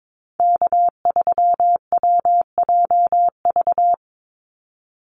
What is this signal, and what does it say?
Morse code, heard as a keyed tone, about 22 wpm, 700 Hz. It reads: K3WJV
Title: Call Signs Repeated 3x - 22wpm